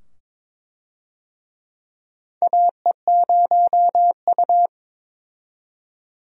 Morse code, heard as a keyed tone, about 22 wpm, 700 Hz. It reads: AE0U
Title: USA Call Signs - 4 Characters - 22wpm